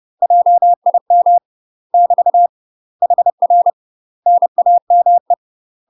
Morse code, heard as a keyed tone, about 30 wpm, 700 Hz. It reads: JIM <BT> HR NAME
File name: UK MASTER QSOs TRAINING / 30WPM / UK MASTER QSOs 0004